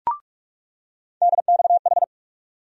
Morse code, heard as a keyed tone, about 42 wpm, 700 Hz. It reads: DXH
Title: USA Call Sign Postfixes - 3 Letters 45wpm